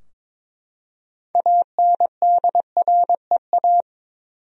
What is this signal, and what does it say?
Morse code, heard as a keyed tone, about 22 wpm, 700 Hz. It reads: ANDREA
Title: USA Common Names - 22wpm